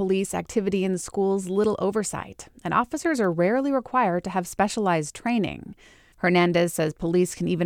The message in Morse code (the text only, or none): none